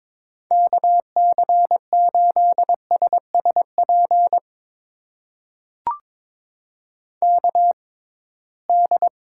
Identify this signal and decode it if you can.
Morse code, heard as a keyed tone, about 23 wpm, spaced as for 17 wpm, 700 Hz. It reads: KC8SSP KD
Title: USA Call Signs - 6 Characters - 6x Character Spacing - 22wpm